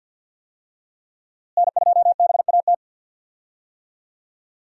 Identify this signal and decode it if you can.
Morse code, heard as a keyed tone, about 51 wpm, 700 Hz. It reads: N2BAT
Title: USA Call Signs - 5 Characters - Repeated 2x - 50wpm